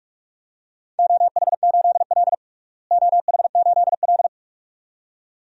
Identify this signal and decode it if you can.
Morse code, heard as a keyed tone, about 46 wpm, 700 Hz. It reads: OH8L OH8L
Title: DXCC-100 Call Signs - 4 Characters - Repeated 2x - 45wpm